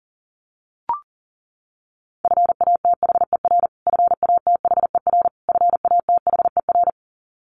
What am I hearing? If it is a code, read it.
Morse code, heard as a keyed tone, about 40 wpm, 700 Hz. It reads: FATHER FATHER FATHER
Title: Top 500 Words Repeated 40wpm